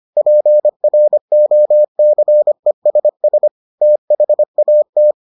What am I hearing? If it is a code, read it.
Morse code, heard as a keyed tone, about 25 wpm, 600 Hz. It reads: PROCESS THAT